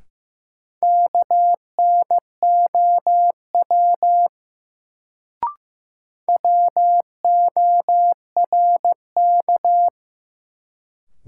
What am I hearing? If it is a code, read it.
Morse code, heard as a keyed tone, about 15 wpm, 700 Hz. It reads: KNOW WORK
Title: Sets of 2 Words - New Word Drill - 15wpm